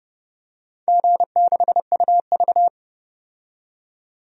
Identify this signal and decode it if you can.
Morse code, heard as a keyed tone, about 31 wpm, 700 Hz. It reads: G6UV
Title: DXCC-100 Call Signs - 4 Characters - 30wpm